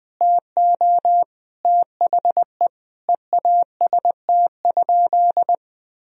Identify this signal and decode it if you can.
Morse code, heard as a keyed tone, about 21 wpm, 700 Hz. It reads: TO THE EAST?
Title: Sentences from Top 1000 Words - Review Part 3 - 20wpm